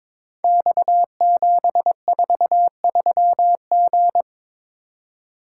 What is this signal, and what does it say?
Morse code, heard as a keyed tone, about 22 wpm, 700 Hz. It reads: X743G
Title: Code Groups - 5 Letter:Numbers - 22wpm